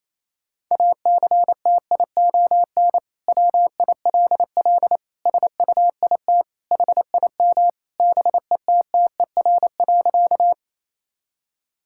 Morse code, heard as a keyed tone, about 28 wpm, 700 Hz. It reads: ACTION WILL SUIT HIM BETTER.
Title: Sentences from Top 1600 Words 28wpm